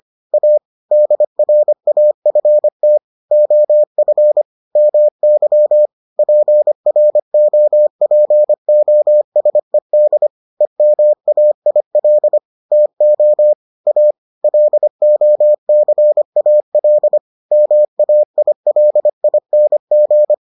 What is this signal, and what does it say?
Morse code, heard as a keyed tone, about 25 wpm, 600 Hz. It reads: A DRAFT OF MY PROPOSED EMAIL TO A LOCAL MAILING